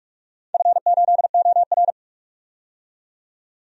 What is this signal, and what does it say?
Morse code, heard as a keyed tone, about 41 wpm, 700 Hz. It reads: F8OR
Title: DXCC-100 Call Signs - 4 Characters - 45wpm